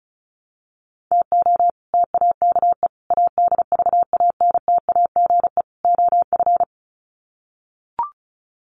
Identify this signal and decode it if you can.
Morse code, heard as a keyed tone, about 34 wpm, 700 Hz. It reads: TO TAKE ADVANTAGE OF
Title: Sets of 4 Words 35wpm